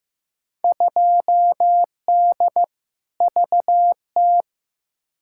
Morse code, heard as a keyed tone, about 15 wpm, 700 Hz. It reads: 2D VT
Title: Field Day Exchanges Encore - 15wpm